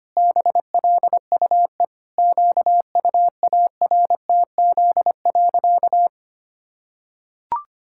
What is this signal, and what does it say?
Morse code, heard as a keyed tone, about 25 wpm, 700 Hz. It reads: BLUE QUARTZ.
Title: Pangram Challenge v2 - 25wpm